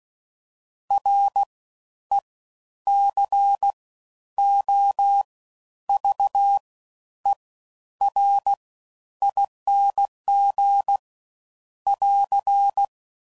Morse code, heard as a keyed tone, about 16 wpm, spaced as for 10 wpm, 800 Hz. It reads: RECOVERING <AR>
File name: Suffixes / Special INGs / Fast Ings